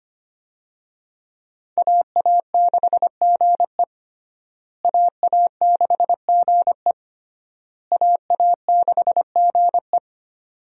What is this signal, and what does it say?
Morse code, heard as a keyed tone, about 25 wpm, 700 Hz. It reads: AA6GE AA6GE AA6GE
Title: USA Call Signs - 5 Characters - Repeated 3x - 25wpm